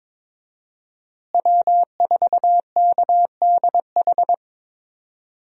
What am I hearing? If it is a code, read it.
Morse code, heard as a keyed tone, about 22 wpm, 700 Hz. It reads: W4KDH
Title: USA Call Signs - 5 Characters - Repeated 2x - 22wpm